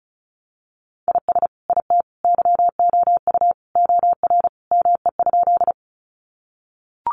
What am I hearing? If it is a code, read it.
Morse code, heard as a keyed tone, about 34 wpm, 700 Hz. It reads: IS IT YOU OR ME?
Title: Sentences from Top 200 Words 35wpm